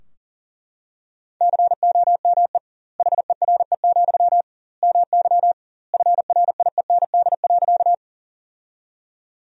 Morse code, heard as a keyed tone, about 40 wpm, 700 Hz. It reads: COME HERE, MY FRIEND.
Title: 2x Word-Spacing - Sentences from Top 300 Words - 40wpm